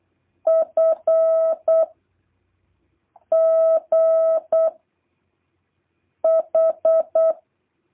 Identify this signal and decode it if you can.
Morse code, heard as a keyed tone, about 8 wpm, 660 Hz. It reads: F G H